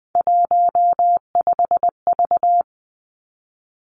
Morse code, heard as a keyed tone, about 20 wpm, 700 Hz. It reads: 15V
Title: Three Challenging Letter-Number Combinations 20wpm